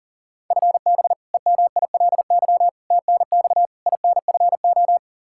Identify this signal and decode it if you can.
Morse code, heard as a keyed tone, about 37 wpm, 700 Hz. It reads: FB EMILY TNX INFO
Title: Practice QSO - Exchange 3 - More Info - 40wpm